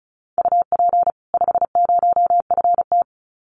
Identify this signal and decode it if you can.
Morse code, heard as a keyed tone, about 34 wpm, 700 Hz. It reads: UP 50FT
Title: Common QSO Phrases 35wpm